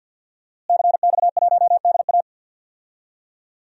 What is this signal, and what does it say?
Morse code, heard as a keyed tone, about 52 wpm, 700 Hz. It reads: CX1DA
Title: DXCC-100 Call Signs - 5 Characters - Repeated 3x - 50wpm